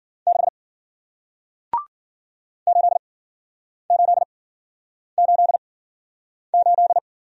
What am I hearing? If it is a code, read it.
Morse code, heard as a keyed tone, about 53 wpm, spaced as for 9 wpm, 700 Hz. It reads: 6 8888